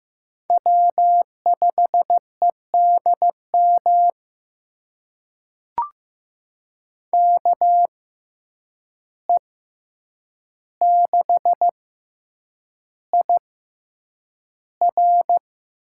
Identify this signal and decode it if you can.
Morse code, heard as a keyed tone, about 15 wpm, spaced as for 8 wpm, 700 Hz. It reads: W5EDM KE6IR
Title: USA Call Signs - 5 Characters - 6x Character Spacing - 15wpm